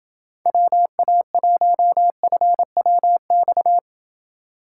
Morse code, heard as a keyed tone, about 28 wpm, 700 Hz. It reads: WA1FWX